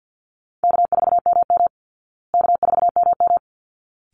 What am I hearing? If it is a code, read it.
Morse code, heard as a keyed tone, about 49 wpm, 700 Hz. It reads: K4MM K4MM
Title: Call Signs Repeated 3x - 50wpm